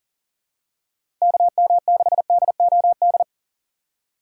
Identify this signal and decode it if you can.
Morse code, heard as a keyed tone, about 40 wpm, 700 Hz. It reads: KM6DOD
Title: USA Call Signs - 6 Characters - 4x Character Spacing - 40wpm